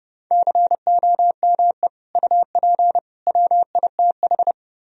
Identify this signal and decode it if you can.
Morse code, heard as a keyed tone, about 28 wpm, 700 Hz. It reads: COME UP WITH